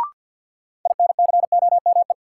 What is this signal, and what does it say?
Morse code, heard as a keyed tone, about 51 wpm, 700 Hz. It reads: INCOME
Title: Sentences from Top 1000 Words - Review Part 5 - New Word Drill - 50wpm